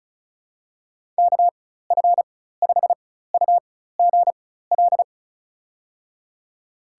Morse code, heard as a keyed tone, about 36 wpm, spaced as for 16 wpm, 700 Hz. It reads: KF5UGL
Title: USA Call Signs - 6 Characters - 4x Character Spacing - 35wpm